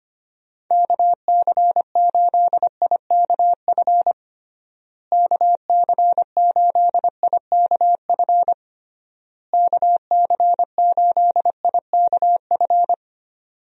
Morse code, heard as a keyed tone, about 25 wpm, 700 Hz. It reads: KC8IKF KC8IKF KC8IKF
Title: USA Call Signs - 6 Characters - Repeated 3x - 25wpm